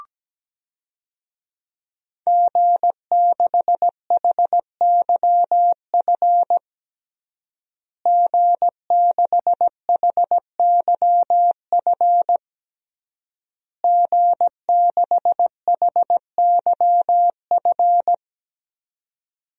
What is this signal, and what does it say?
Morse code, heard as a keyed tone, about 17 wpm, 700 Hz. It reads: G6HYF G6HYF G6HYF
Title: DXCC-100 Call Signs - 5 Characters - Repeated 3x - 17wpm